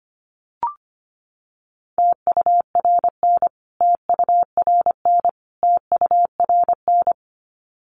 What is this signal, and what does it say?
Morse code, heard as a keyed tone, about 25 wpm, 700 Hz. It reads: TURN TURN TURN
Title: Top 300 Words Repeated 25wpm